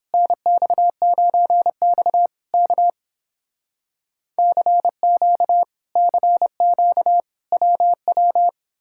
Morse code, heard as a keyed tone, about 30 wpm, 700 Hz. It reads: NX9X K CQ CQ WW